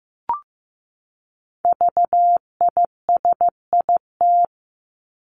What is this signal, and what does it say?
Morse code, heard as a keyed tone, about 15 wpm, 700 Hz. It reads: VISIT
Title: Top 751-1000 Words 15wpm